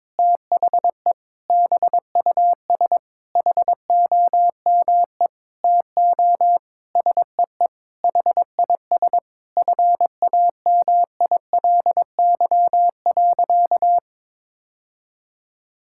Morse code, heard as a keyed tone, about 22 wpm, 700 Hz. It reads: THE BUS HOME TO SEE HIS FAMILY.
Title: Sentences from Top 1000 Words - Review Part 6 - 22wpm